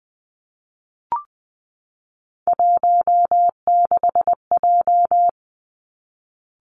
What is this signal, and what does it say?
Morse code, heard as a keyed tone, about 20 wpm, 700 Hz. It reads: 16J
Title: Three Challenging Letter-Number Combinations 20wpm